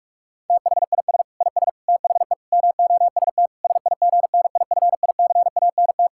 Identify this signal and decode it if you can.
Morse code, heard as a keyed tone, about 46 wpm, 700 Hz. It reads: THIS IS THE MOST SIGNIFICANT
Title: Sentences from Top 1000 Words - Review Part 6 - 45wpm